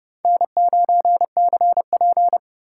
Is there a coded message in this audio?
Yes: Morse code from 0.2 to 2.4 s, heard as a keyed tone, about 28 wpm, 700 Hz. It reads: N9CP